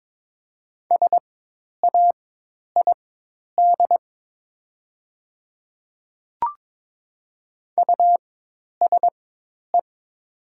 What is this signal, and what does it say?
Morse code, heard as a keyed tone, about 22 wpm, spaced as for 10 wpm, 700 Hz. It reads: SAID USE
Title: 4x Character Spacing - Top 100 Words - 22wpm